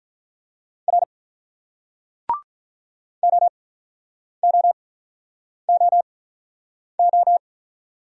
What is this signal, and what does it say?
Morse code, heard as a keyed tone, about 46 wpm, spaced as for 9 wpm, 700 Hz. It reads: R OOOO